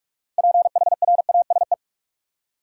Morse code, heard as a keyed tone, about 43 wpm, 700 Hz. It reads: PHRASE
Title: Six Letter Words - 45wpm